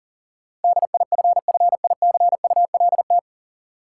Morse code, heard as a keyed tone, about 40 wpm, 700 Hz. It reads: DIFFICULT
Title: Sentences from Top 600 Words - New Word Drill - 40wpm